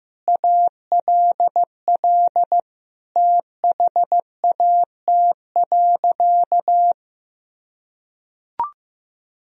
Morse code, heard as a keyed tone, about 15 wpm, 700 Hz. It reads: ALL THAT.
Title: Sentences from Top 1000 Words - Review Part 1 - 15wpm